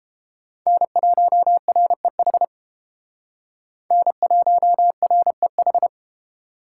Morse code, heard as a keyed tone, about 33 wpm, 700 Hz. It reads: N1REH N1REH